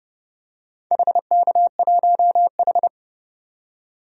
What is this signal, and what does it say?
Morse code, heard as a keyed tone, about 30 wpm, 700 Hz. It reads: HK1H
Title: DXCC-100 Call Signs - 4 Characters - 30wpm